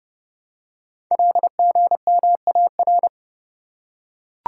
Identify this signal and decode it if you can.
Morse code, heard as a keyed tone, about 30 wpm, 700 Hz. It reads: LGMAR
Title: Code Groups - 5 Letters - 30wpm